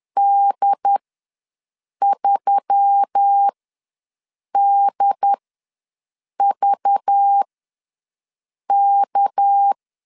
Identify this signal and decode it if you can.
Morse code, heard as a keyed tone, about 10 wpm, spaced as for 6 wpm, 800 Hz. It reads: D3DVK
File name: LEZIONI / 8 YJ45 / 8 YJ ws 10wpm 4043